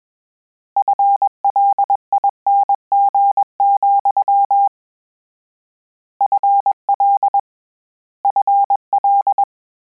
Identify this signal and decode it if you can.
Morse code, heard as a keyed tone, about 21 wpm, 800 Hz. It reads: FLING, FL FL
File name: Combinations / Combinations-mp3s / Combinations@20